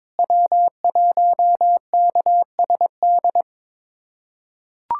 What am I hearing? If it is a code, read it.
Morse code, heard as a keyed tone, about 22 wpm, 700 Hz. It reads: W1KSD
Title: USA Callsign DE Callsign 22wpm